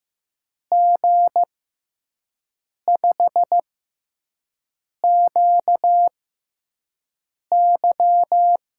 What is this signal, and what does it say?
Morse code, heard as a keyed tone, about 15 wpm, spaced as for 5 wpm, 700 Hz. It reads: G5QY
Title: USA Call Signs - 6x Character Spacing - 15wpm